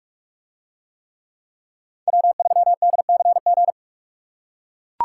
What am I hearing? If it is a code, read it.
Morse code, heard as a keyed tone, about 44 wpm, 700 Hz. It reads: W3DCG